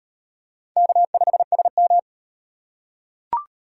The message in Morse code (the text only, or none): K5SM